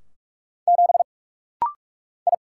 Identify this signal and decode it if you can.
Morse code, heard as a keyed tone, about 46 wpm, 700 Hz. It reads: 7 I
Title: Single Letter-Number - Rapid-Fire 45wpm